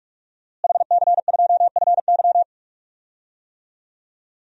Morse code, heard as a keyed tone, about 44 wpm, 700 Hz. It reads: HC2FY
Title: DXCC-100 Call Signs - 5 Characters - Repeated 2x - 45wpm